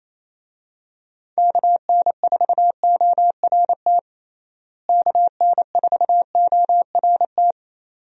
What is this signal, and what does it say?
Morse code, heard as a keyed tone, about 28 wpm, 700 Hz. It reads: KN4ORT KN4ORT
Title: USA Call Signs - 6 Characters - Repeated 2x - 28wpm